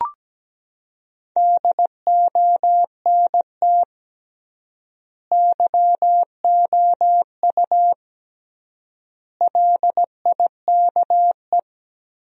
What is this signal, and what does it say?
Morse code, heard as a keyed tone, about 17 wpm, 700 Hz. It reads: DONT YOU LIKE
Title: 3x Word-Spacing - Sentences from Top 300 Words - 17wpm